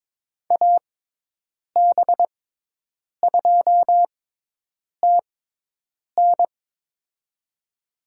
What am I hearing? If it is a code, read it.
Morse code, heard as a keyed tone, about 22 wpm, spaced as for 8 wpm, 700 Hz. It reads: AB2TN
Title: USA Call Signs - 5 Characters - 6x Character Spacing - 22wpm